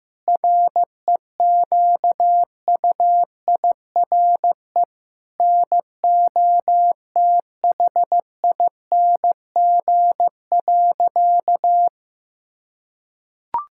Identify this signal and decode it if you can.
Morse code, heard as a keyed tone, about 15 wpm, 700 Hz. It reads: REQUIRE NOTHING.
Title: Sentences from Top 1000 Words - Review Part 6 - 15wpm